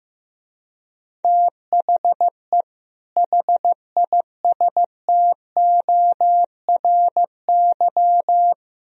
Morse code, heard as a keyed tone, about 15 wpm, 700 Hz. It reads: THE HISTORY
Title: Speed-Racing - Sets of 3 Words 15wpm